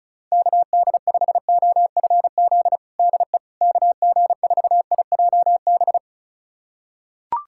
Morse code, heard as a keyed tone, about 36 wpm, 700 Hz. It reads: KD5OFZ DE KG4IJB
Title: USA Callsign DE Callsign - 6 Characters - 35wpm